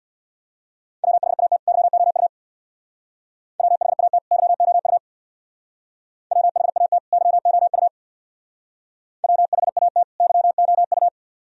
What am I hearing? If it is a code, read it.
Morse code, heard as a keyed tone, about 62 wpm, 700 Hz. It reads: WHAT YOU WHAT YOU WHAT YOU WHAT YOU
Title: Speed-Racing - Sets of 2 Words 50wpm